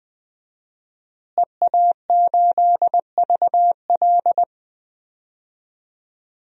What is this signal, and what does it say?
Morse code, heard as a keyed tone, about 20 wpm, 700 Hz. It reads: EA8VL
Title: DXCC-100 Call Signs - 5 Characters - Repeated 3x - 20wpm